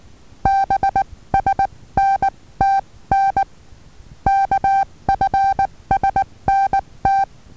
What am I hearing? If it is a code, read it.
Morse code, heard as a keyed tone, about 19 wpm, spaced as for 16 wpm, 780 Hz. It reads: BSNTN KFSNT
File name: Taviro HA6VA B